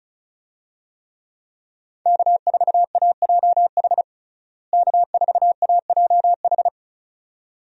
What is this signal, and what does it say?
Morse code, heard as a keyed tone, about 34 wpm, 700 Hz. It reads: K4AJH K4AJH
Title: USA Call Signs - 5 Characters - Repeated 2x - 35wpm